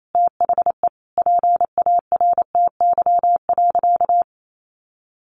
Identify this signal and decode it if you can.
Morse code, heard as a keyed tone, about 28 wpm, 700 Hz. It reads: THE PARTY.
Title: Sentences from Top 1400 Words 28wpm